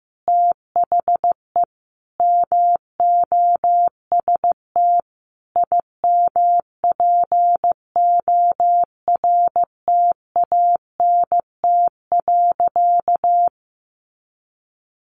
Morse code, heard as a keyed tone, about 15 wpm, 700 Hz. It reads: THE MOST IMPORTANT.
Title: Sentences from Top 700 Words 15wpm